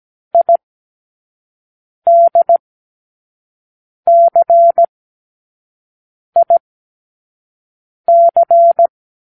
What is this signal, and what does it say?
Morse code, heard as a keyed tone, about 17 wpm, spaced as for 5 wpm, 680 Hz. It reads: IDCIC
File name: k7qo / Lesson18(A-L)